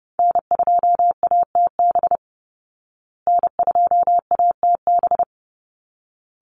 Morse code, heard as a keyed tone, about 30 wpm, 700 Hz. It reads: N2ATB N2ATB
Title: Call Signs Repeated 3x - 30wpm